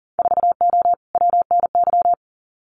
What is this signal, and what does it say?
Morse code, heard as a keyed tone, about 40 wpm, 700 Hz. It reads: 4O WNY
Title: Winter Field Day Exchanges Encore 40wpm